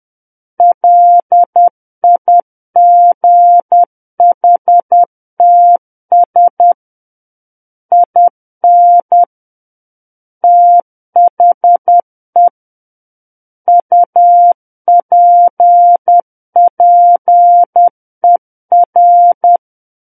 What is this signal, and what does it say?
Morse code, heard as a keyed tone, about 10 wpm, 700 Hz. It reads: LIGHTS IN THE UPPER
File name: Book One / 01-Chapter 1 - 10 WPM